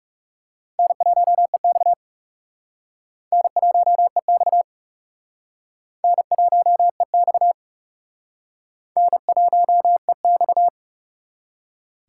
Morse code, heard as a keyed tone, about 40 wpm, 700 Hz. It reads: N1EX N1EX N1EX N1EX